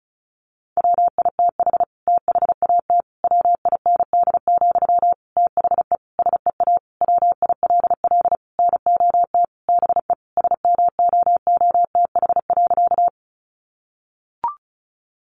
Morse code, heard as a keyed tone, about 34 wpm, 700 Hz. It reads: WITH THAT WIND, THE SEA WILL NOT BE SMOOTH.
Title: Sentences from Top 1400 Words 35wpm